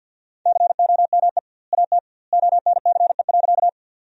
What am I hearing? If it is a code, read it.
Morse code, heard as a keyed tone, about 52 wpm, 700 Hz. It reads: COME AT ONCE.
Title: 2x Word-Spacing - Sentences from Top 300 Words - 50wpm